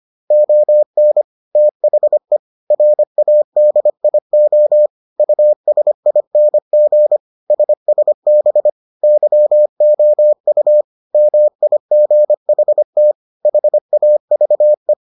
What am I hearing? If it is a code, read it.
Morse code, heard as a keyed tone, about 25 wpm, 600 Hz. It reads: ON THE RADIO USING SSB YOU MIGHT HAVE